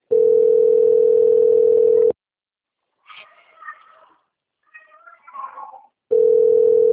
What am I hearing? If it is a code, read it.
Ringback tone (North America), 440 + 480 Hz.